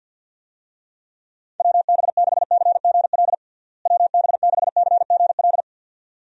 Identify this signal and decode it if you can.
Morse code, heard as a keyed tone, about 51 wpm, 700 Hz. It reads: WB6CGL WB6CGL
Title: USA Call Signs - 6 Characters - Repeated 2x - 50wpm